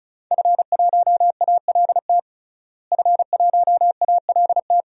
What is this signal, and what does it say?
Morse code, heard as a keyed tone, about 36 wpm, 700 Hz. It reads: F1ALT F1ALT
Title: DXCC-100 Call Signs - 5 Characters - Repeated 3x - 35wpm